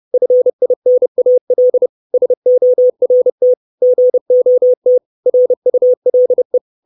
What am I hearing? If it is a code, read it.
Morse code, heard as a keyed tone, about 30 wpm, 500 Hz. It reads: FINAL SORT GOT RULE
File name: 632 Words 30WPM